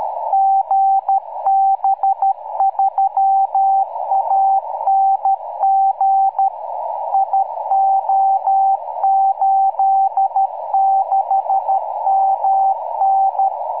Morse code, heard as a keyed tone, about 13 wpm, 780 Hz, with radio noise. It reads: GB3ANG IO86MN
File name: iw0ffk / 2006 06 03 GB3ANG beacon